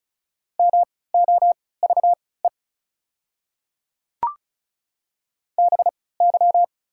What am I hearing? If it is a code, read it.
Morse code, heard as a keyed tone, about 34 wpm, spaced as for 20 wpm, 690 Hz. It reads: MOVE BY
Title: 3x Character Spacing - Top 300 Words - 35wpm